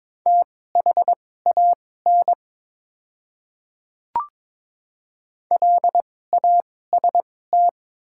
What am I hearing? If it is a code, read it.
Morse code, heard as a keyed tone, about 22 wpm, spaced as for 16 wpm, 690 Hz. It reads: THAN LAST
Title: 2x Character Spacing - Top 500 Words - 22wpm